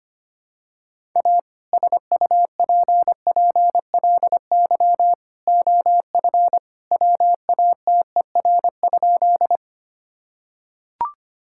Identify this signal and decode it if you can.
Morse code, heard as a keyed tone, about 25 wpm, 700 Hz. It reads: A SUPPLY OF WATER?
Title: Sentences from Top 1000 Words - Review Part 5 - 25wpm